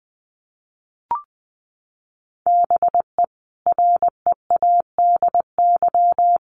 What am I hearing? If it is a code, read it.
Morse code, heard as a keyed tone, about 20 wpm, 700 Hz. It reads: BE READY
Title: Sentences from Top 1100 Words 20wpm